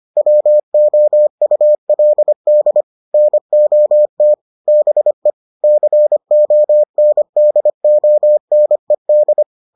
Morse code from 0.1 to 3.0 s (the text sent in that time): WOULD